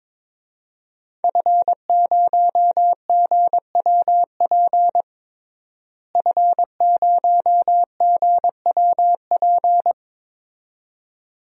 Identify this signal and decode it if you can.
Morse code, heard as a keyed tone, about 22 wpm, 700 Hz. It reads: F0GWP F0GWP
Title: DXCC-100 Call Signs - 5 Characters - Repeated 2x - 22wpm